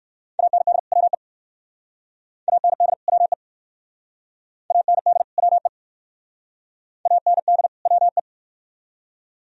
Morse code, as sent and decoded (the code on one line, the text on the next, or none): .- -. -.. / .-- . / .- -. -.. / .-- . / .- -. -.. / .-- . / .- -. -.. / .-- .
AND WE AND WE AND WE AND WE